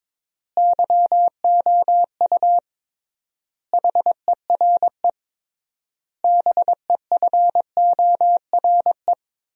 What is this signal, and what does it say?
Morse code, heard as a keyed tone, about 22 wpm, 700 Hz. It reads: YOU HERE BEFORE
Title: 3x Word-Spacing - Sentences from Top 200 Words - 22wpm